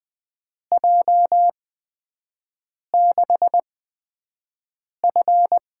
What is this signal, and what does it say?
Morse code, heard as a keyed tone, about 20 wpm, spaced as for 5 wpm, 700 Hz. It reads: J6F
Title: USA Call Signs - 6 Characters - 8x Character Spacing - 20wpm